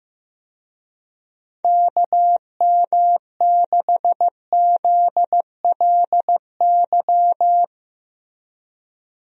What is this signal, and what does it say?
Morse code, heard as a keyed tone, about 15 wpm, 700 Hz. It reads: KM6ZLY
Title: USA Call Signs - 6 Characters - Repeated 2x - 15wpm